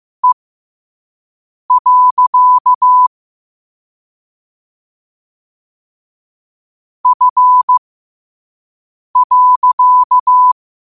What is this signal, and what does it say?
Morse code, heard as a keyed tone, about 15 wpm, spaced as for 5 wpm, 1000 Hz. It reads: E. F.
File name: qotdfiles / 5wpm